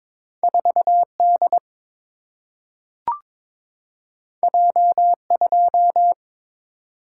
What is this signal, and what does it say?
Morse code, heard as a keyed tone, about 22 wpm, 700 Hz. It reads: 4D J2
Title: Single-Letter Single-Number 22wpm